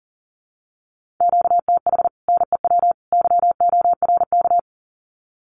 Morse code, heard as a keyed tone, about 40 wpm, 700 Hz. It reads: QTH NEW YORK
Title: Common QSO Phrases 40wpm